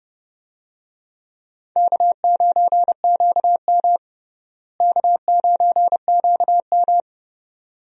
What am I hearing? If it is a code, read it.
Morse code, heard as a keyed tone, about 30 wpm, 700 Hz. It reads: K9QM K9QM
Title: USA Call Signs - 4 Characters - Repeated 2x - 30wpm